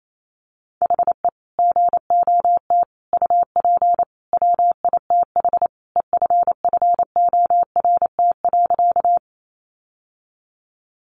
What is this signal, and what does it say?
Morse code, heard as a keyed tone, about 28 wpm, 700 Hz. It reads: HE GOT UP WITH EFFORT.
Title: Sentences from Top 1200 Words 28wpm